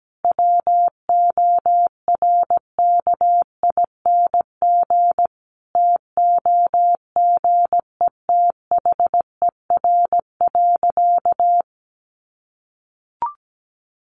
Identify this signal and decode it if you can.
Morse code, heard as a keyed tone, about 17 wpm, 700 Hz. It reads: WORKING TOGETHER.
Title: Sentences from Top 1200 Words 17wpm